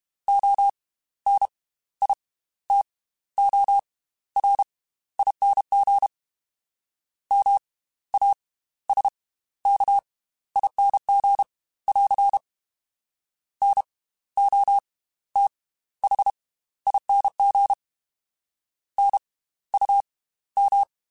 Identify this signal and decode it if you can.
Morse code, heard as a keyed tone, about 32 wpm, spaced as for 15 wpm, 800 Hz. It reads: ONITORING MASKING <AR> NOTHING NUM